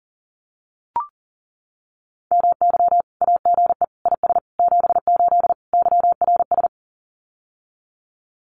Morse code, heard as a keyed tone, about 38 wpm, 700 Hz. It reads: MY AGE IS 78 YRS
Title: Common QSO Phrases Encore 40wpm